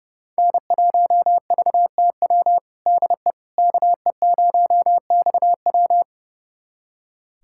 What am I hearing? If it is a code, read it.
Morse code, heard as a keyed tone, about 30 wpm, 700 Hz. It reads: N1VTW DE KE0XW
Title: USA Callsign DE Callsign - 5 Characters - 30wpm